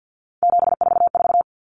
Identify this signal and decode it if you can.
Morse code, heard as a keyed tone, about 50 wpm, 700 Hz. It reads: 744
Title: Numbers - 3-Digit 50wpm